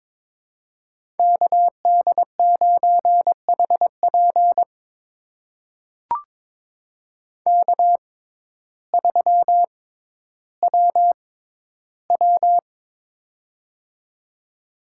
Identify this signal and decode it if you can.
Morse code, heard as a keyed tone, about 22 wpm, spaced as for 12 wpm, 700 Hz. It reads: KD9HP K3WW
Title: USA Call Signs - 6x Character Spacing - 22wpm